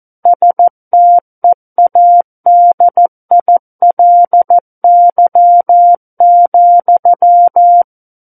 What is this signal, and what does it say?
Morse code, heard as a keyed tone, about 14 wpm, 700 Hz. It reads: STEADILY,
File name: Book One / 05-Chapter 5 - 14 WPM